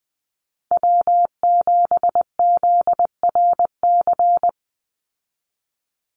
Morse code, heard as a keyed tone, about 20 wpm, 700 Hz. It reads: W7ZRC
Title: Call Signs Repeated 3x - 20wpm